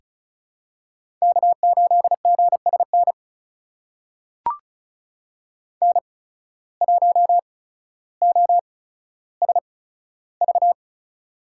Morse code, heard as a keyed tone, about 36 wpm, spaced as for 15 wpm, 700 Hz. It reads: K8GSN N1OSV
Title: USA Call Signs - 5 Characters - 8x Character Spacing - 35wpm